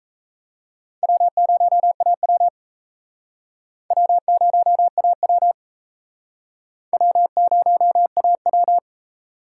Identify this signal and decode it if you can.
Morse code, heard as a keyed tone, about 35 wpm, 700 Hz. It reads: W0AW W0AW W0AW